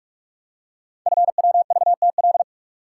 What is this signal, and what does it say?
Morse code, heard as a keyed tone, about 44 wpm, 700 Hz. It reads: FWVTL
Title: Code Groups - 5 Letter:Numbers - 45wpm